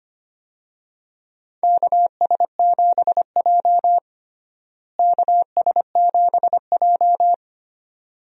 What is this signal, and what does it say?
Morse code, heard as a keyed tone, about 25 wpm, 700 Hz. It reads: KS7J KS7J
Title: USA Call Signs - 4 Characters - Repeated 2x - 25wpm